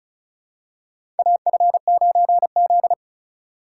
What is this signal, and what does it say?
Morse code, heard as a keyed tone, about 36 wpm, 700 Hz. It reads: AF9Z
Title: USA Call Signs - 4 Characters - 2x Character Spacing - 35wpm